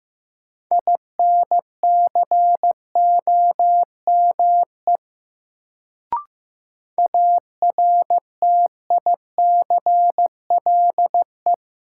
Morse code, heard as a keyed tone, about 15 wpm, 700 Hz. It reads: INCOME ARTICLE
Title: Sentences from Top 1000 Words - Review Part 5 - New Word Drill - 15wpm